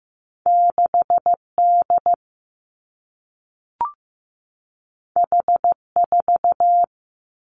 Morse code, heard as a keyed tone, about 15 wpm, 700 Hz. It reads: 6D H4
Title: Two Challenging Letter-Number Combinations 15wpm